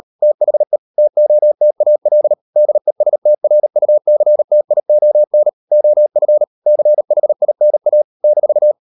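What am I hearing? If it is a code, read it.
Morse code, heard as a keyed tone, about 40 wpm, 600 Hz. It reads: THE TOTAL DESTRUCTION OF CHINA -
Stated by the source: Morse code practice at 35–39 wpm sending English text of Ken Follett's novel Never, from CW med Gnister.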